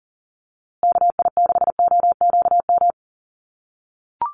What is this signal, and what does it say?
Morse code, heard as a keyed tone, about 40 wpm, 700 Hz. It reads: KI6OQM
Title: USA Call Signs Repeated 3x 40wpm